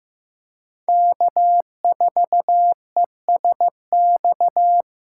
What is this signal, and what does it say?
Morse code, heard as a keyed tone, about 15 wpm, 700 Hz. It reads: K4ESX
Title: USA Call Signs - 3x Character Spacing - 15wpm